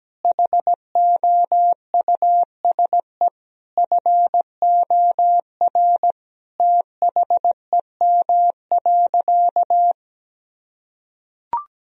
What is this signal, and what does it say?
Morse code, heard as a keyed tone, about 17 wpm, 700 Hz. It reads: HOUSE FOR THEM.
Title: Sentences from Top 1000 Words - Review Part 1 - 17wpm